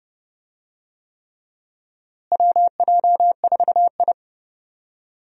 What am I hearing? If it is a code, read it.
Morse code, heard as a keyed tone, about 31 wpm, 700 Hz. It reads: WJ4I
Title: USA Call Signs - 4 Characters - Repeated 3x - 30wpm